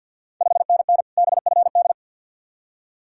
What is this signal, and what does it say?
Morse code, heard as a keyed tone, about 51 wpm, 690 Hz. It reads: 5NN BFD
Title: Ontario QSO Party Exchange 50wpm